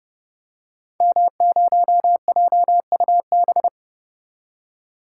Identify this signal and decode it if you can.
Morse code, heard as a keyed tone, about 30 wpm, 700 Hz. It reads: M0JUB
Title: Code Groups - 5 Letter:Numbers - 30wpm